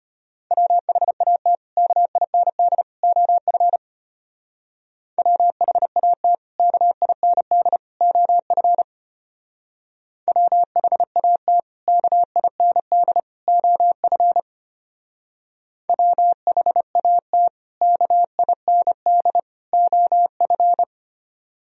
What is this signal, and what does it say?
Morse code, heard as a keyed tone, about 31 wpm, 700 Hz. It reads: WHAT KIND OF WHAT KIND OF WHAT KIND OF WHAT KIND OF
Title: Speed-Racing - Sets of 3 Words 25wpm